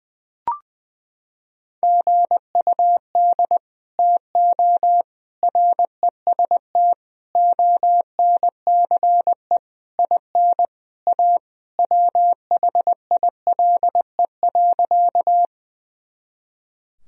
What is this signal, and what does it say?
Morse code, heard as a keyed tone, about 20 wpm, 700 Hz. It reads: GUD TO REST ONCE IN A WHILE.
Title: Sentences from Top 500 Words and Abbreviations 20wpm